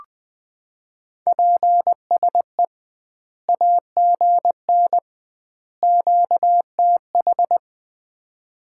Morse code, heard as a keyed tone, about 20 wpm, 700 Hz. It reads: PSE AGN QTH
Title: Common QSO Phrases - 2x Word-Spacing - 20wpm